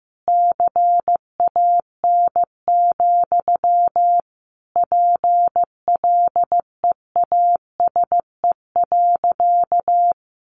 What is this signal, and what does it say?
Morse code, heard as a keyed tone, about 15 wpm, 700 Hz. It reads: CAN, PLEASE.
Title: Sentences from Top 1500 Words 15wpm